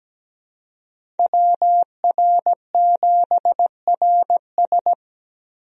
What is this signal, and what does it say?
Morse code, heard as a keyed tone, about 17 wpm, 700 Hz. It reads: WR7RS
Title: USA Call Signs - 5 Characters - Repeated 3x - 17wpm